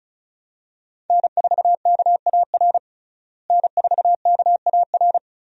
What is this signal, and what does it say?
Morse code, heard as a keyed tone, about 34 wpm, 700 Hz. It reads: N4KAR N4KAR
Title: USA Call Signs - 5 Characters - Repeated 2x - 35wpm